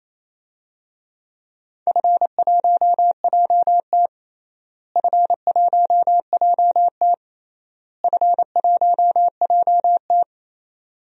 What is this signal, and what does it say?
Morse code, heard as a keyed tone, about 28 wpm, 700 Hz. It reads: F1JT F1JT F1JT
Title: DXCC-100 Call Signs - 4 Characters - Repeated 3x - 28wpm